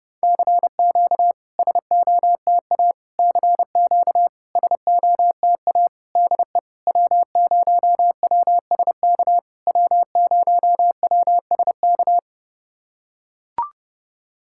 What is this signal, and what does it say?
Morse code, heard as a keyed tone, about 30 wpm, 700 Hz. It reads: CQ SOTA CQ SOTA DE W0WSK W0WSK
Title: SOTA QSO Practice 30wpm